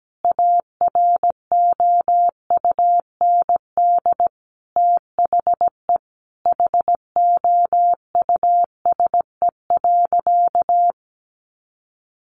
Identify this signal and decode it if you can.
Morse code, heard as a keyed tone, about 17 wpm, 700 Hz. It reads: AROUND THE HOUSE.
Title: Sentences from Top 600 Words 17wpm